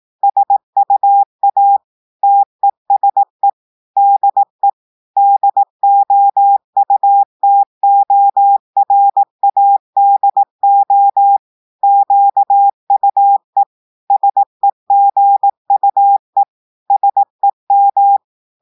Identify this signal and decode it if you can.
Morse code, heard as a keyed tone, about 18 wpm, 800 Hz. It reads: SUA TESE DE DOUTORADO QUE SEGUE SEM
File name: audios / portugais / portugais texte en clair 1 vitesse 18 mots minutes